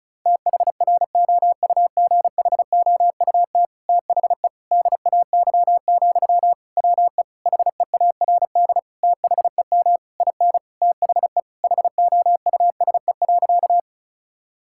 Morse code, heard as a keyed tone, about 34 wpm, 700 Hz. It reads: THROUGHOUT THE DAY, WE HEARD THEM IN THE HOUSE.
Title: Sentences from Top 1000 Words - Review Part 5 - 35wpm